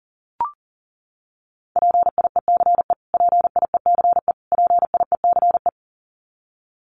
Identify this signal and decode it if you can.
Morse code, heard as a keyed tone, about 37 wpm, 700 Hz. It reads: PIECE PIECE PIECE
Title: Top 500 Words Repeated 40wpm